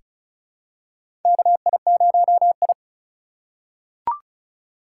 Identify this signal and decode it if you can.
Morse code, heard as a keyed tone, about 34 wpm, 700 Hz. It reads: KI0I
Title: K1USN SST QSO Practice - 35wpm